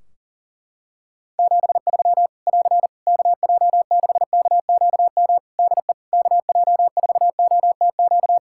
Morse code, heard as a keyed tone, about 43 wpm, 700 Hz. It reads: 73 <AR> KJ6KQM DE KJ4OTQ
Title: Practice QSO - Exchange 4 - Ending - 40wpm